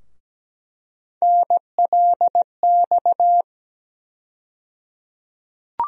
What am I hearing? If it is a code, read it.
Morse code, heard as a keyed tone, about 17 wpm, 700 Hz. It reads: NLX
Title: USA Call Sign Postfixes - 3 Letters 17wpm